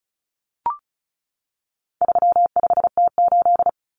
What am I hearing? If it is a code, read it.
Morse code, heard as a keyed tone, about 34 wpm, 700 Hz. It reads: 35T8
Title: Common Cut Numbers - 4-Digit 35wpm